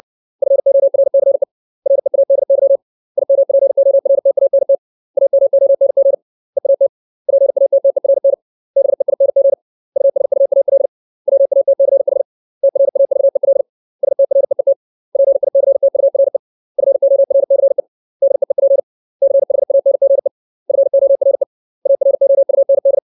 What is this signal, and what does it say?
Morse code, heard as a keyed tone, about 60 wpm, 550 Hz. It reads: FORCE READY IMPORTANT AMONG EAT PATTERN BEING USUAL WATCH TRAVEL STREET PICTURE FORCE DEEP CHANGE FORE AROUND
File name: 632 words / 632 Words 60WPM